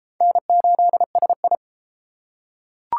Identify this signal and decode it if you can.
Morse code, heard as a keyed tone, about 28 wpm, 700 Hz. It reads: N8SI